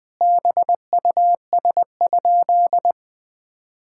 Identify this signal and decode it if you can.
Morse code, heard as a keyed tone, about 20 wpm, 700 Hz. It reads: BUS?
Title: Sentences from Top 1000 Words - Review Part 6 - 20wpm